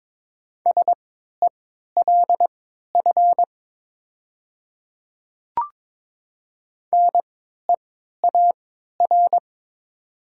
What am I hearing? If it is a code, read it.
Morse code, heard as a keyed tone, about 22 wpm, spaced as for 12 wpm, 700 Hz. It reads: SELF NEAR
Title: 3x Character Spacing - Top 500 Words - 22wpm